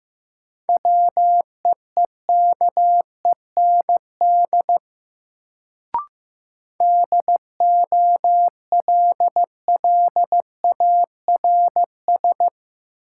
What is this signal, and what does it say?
Morse code, heard as a keyed tone, about 15 wpm, 700 Hz. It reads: WEEKEND DOLLARS
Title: Sets of 5 Words - New Word Drill - 15wpm